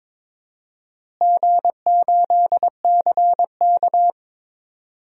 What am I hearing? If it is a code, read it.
Morse code, heard as a keyed tone, about 22 wpm, 700 Hz. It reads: G8CK
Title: DXCC-100 Call Signs - 4 Characters - Repeated 3x - 22wpm